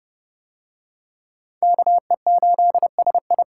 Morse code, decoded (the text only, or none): KE8SI